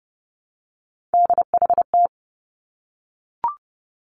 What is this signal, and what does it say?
Morse code, heard as a keyed tone, about 30 wpm, 700 Hz. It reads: DHT
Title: Three Challenging Letter-Number Combinations 30wpm